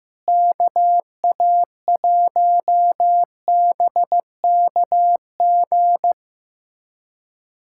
Morse code, heard as a keyed tone, about 15 wpm, 700 Hz. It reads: KA1BKG
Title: USA Callsign DE Callsign - 6 Characters - 15wpm